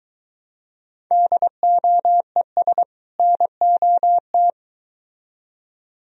Morse code, heard as a keyed tone, about 23 wpm, 700 Hz. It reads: DOES NOT